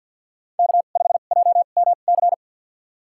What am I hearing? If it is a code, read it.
Morse code, heard as a keyed tone, about 50 wpm, spaced as for 36 wpm, 700 Hz. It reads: K5JMC
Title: USA Call Signs - 2x Character Spacing - 50wpm